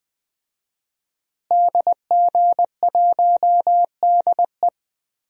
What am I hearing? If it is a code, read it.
Morse code, heard as a keyed tone, about 21 wpm, 700 Hz. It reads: DG1DE
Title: DXCC-100 Call Signs - 5 Characters - Repeated 3x - 20wpm